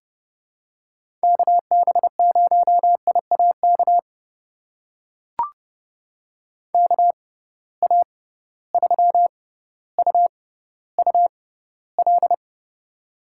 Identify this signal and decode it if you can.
Morse code, heard as a keyed tone, about 32 wpm, spaced as for 16 wpm, 700 Hz. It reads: KB0IAK KA3UUL
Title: USA Call Signs - 6x Character Spacing - 30wpm